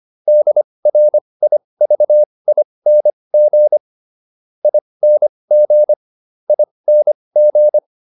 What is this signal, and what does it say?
Morse code, heard as a keyed tone, about 25 wpm, spaced as for 20 wpm, 600 Hz. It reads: DRIVING ING ING